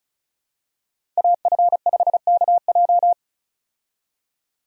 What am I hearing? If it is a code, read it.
Morse code, heard as a keyed tone, about 34 wpm, 700 Hz. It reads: AF5KJ
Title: USA Call Signs - 5 Characters - 35wpm